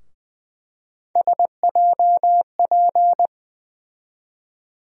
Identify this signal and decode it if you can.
Morse code, heard as a keyed tone, about 20 wpm, 700 Hz. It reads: SJP
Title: USA Call Sign Postfixes - 3 Letters 20wpm